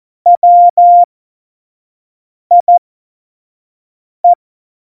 Morse code, heard as a keyed tone, about 14 wpm, spaced as for 5 wpm, 700 Hz. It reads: WIE
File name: Morsekurs / Witze / Witz 19. November 2019, 25 F70